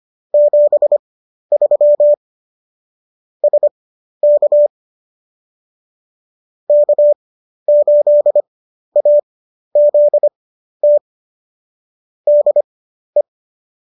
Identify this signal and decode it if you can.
Morse code, heard as a keyed tone, about 25 wpm, spaced as for 12 wpm, 590 Hz. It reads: 73 SK K8AZT DE